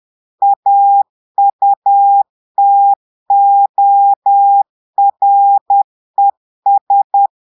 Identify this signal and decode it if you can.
Morse code, heard as a keyed tone, about 10 wpm, 800 Hz. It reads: AUTORES
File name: Portugais - texte en clair 1 - vitesse - 10 mots minute